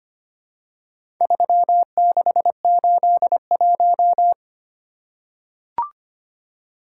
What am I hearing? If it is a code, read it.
Morse code, heard as a keyed tone, about 25 wpm, 700 Hz. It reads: 3681
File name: practice / 25wpm / Four Letter-Number Combinations - 25wpm